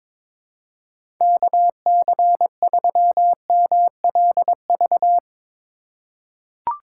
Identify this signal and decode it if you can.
Morse code, heard as a keyed tone, about 22 wpm, 700 Hz. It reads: KC3MLV
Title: USA Call Signs - 6 Characters - 6x Character Spacing - 22wpm